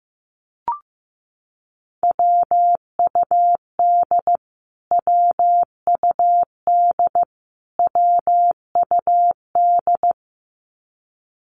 Morse code, heard as a keyed tone, about 15 wpm, 700 Hz. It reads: WUD WUD WUD
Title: Abbreviations Repeated 3x - 15wpm